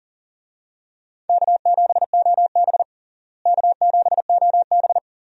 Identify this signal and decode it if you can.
Morse code, heard as a keyed tone, about 40 wpm, 700 Hz. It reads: K7OB K7OB
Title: USA Call Signs - 4 Characters - Repeated 2x - 40wpm